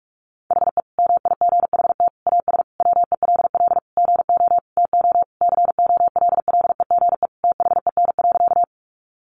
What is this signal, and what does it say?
Morse code, heard as a keyed tone, about 44 wpm, 700 Hz. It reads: HE MIGHT AS WELL GO TO COLLEGE THEN.
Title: Sentences from Top 1300 Words 45wpm